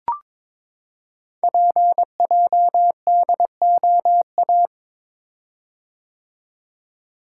Morse code, heard as a keyed tone, about 22 wpm, 700 Hz. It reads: PJDOA
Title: Code Groups - 5 Letters - 22wpm